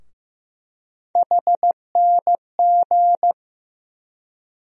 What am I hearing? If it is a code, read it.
Morse code, heard as a keyed tone, about 15 wpm, 700 Hz. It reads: HNG
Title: USA Call Sign Postfixes - 3 Letters 15wpm